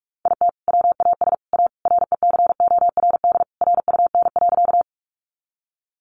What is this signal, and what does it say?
Morse code, heard as a keyed tone, about 47 wpm, 700 Hz. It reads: IT WAS A RECORD RUN.
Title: Sentences from Top 500 Words 45wpm